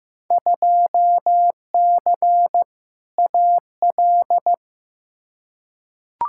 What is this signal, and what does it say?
Morse code, heard as a keyed tone, about 16 wpm, 700 Hz. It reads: 2C AL
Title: Field Day Exchanges Encore - 15wpm